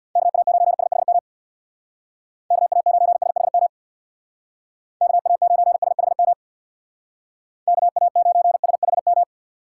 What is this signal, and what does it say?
Morse code, heard as a keyed tone, about 64 wpm, 700 Hz. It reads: KA9SHM KA9SHM KA9SHM KA9SHM